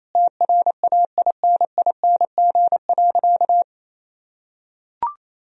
Morse code, heard as a keyed tone, about 28 wpm, 700 Hz. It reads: TRAINING.
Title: Sentences from Top 1000 Words - Review Part 4 - 28wpm